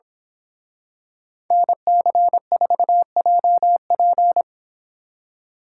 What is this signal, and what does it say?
Morse code, heard as a keyed tone, about 26 wpm, 700 Hz. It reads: NC4JP
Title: Speed Racing - USA 2x2 Call Signs 22wpm